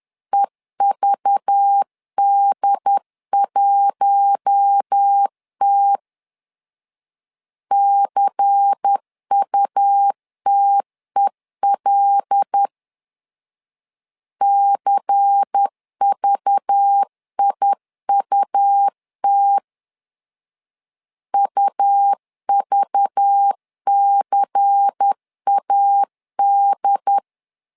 Morse code, heard as a keyed tone, about 10 wpm, 800 Hz. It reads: EVD1T CUTEL CVIUT UVCAD
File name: LEZIONI / 5 LBVC / 5 LBVC 10wpm 2349